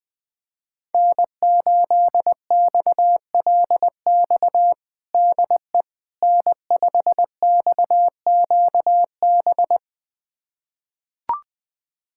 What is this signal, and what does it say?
Morse code, heard as a keyed tone, about 21 wpm, 700 Hz. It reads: N8XLX DE N5XQB
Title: USA Callsign DE Callsign - 5 Characters - 20wpm